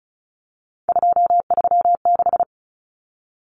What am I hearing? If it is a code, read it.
Morse code, heard as a keyed tone, about 34 wpm, 700 Hz. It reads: 236
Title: Numbers - 3-Digit 35wpm